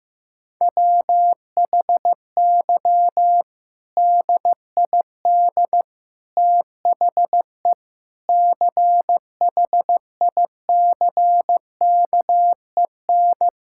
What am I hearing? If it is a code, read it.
Morse code, heard as a keyed tone, about 15 wpm, 700 Hz. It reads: WHY DID THE CHICKEN
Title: Sentences from Top 1000 Words - Review Part 5 - 15wpm